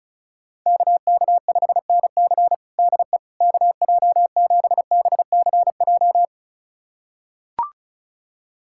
Morse code, heard as a keyed tone, about 34 wpm, 700 Hz. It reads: KK5NC DE KJ7BCJ
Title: USA Callsign DE Callsign 35wpm